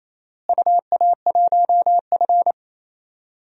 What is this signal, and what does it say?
Morse code, heard as a keyed tone, about 28 wpm, 700 Hz. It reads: UA1F
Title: DXCC-100 Call Signs - 4 Characters - Repeated 2x - 28wpm